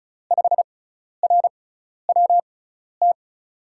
Morse code, heard as a keyed tone, about 36 wpm, spaced as for 12 wpm, 700 Hz. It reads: 5RWT
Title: USA Call Signs - 5 Characters - 6x Character Spacing - 35wpm